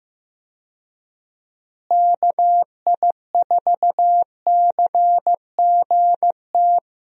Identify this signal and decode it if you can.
Morse code, heard as a keyed tone, about 15 wpm, 700 Hz. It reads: KI4CGT
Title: USA Call Signs - 6 Characters - Repeated 2x - 15wpm